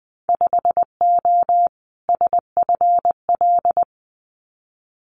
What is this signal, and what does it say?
Morse code, heard as a keyed tone, about 20 wpm, 700 Hz. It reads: 5O SFL
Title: Winter Field Day Exchanges 20wpm